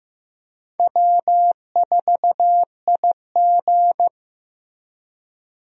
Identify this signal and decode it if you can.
Morse code, heard as a keyed tone, about 15 wpm, 700 Hz. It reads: W4IG
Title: USA Call Signs - 4 Characters - Repeated 3x - 15wpm